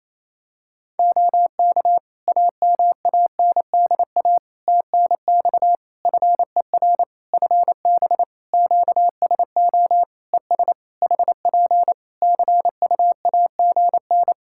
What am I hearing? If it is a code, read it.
Morse code, heard as a keyed tone, about 27 wpm, 700 Hz. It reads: OK AMANDA TNX FER FB QSO ES HP CUAGN
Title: Practice QSO - Exchange 4 - Ending - 28wpm